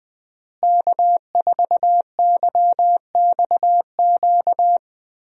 Morse code, heard as a keyed tone, about 20 wpm, 700 Hz. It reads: K4YXQ
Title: USA Call Signs - 5 Characters - Repeated 3x - 20wpm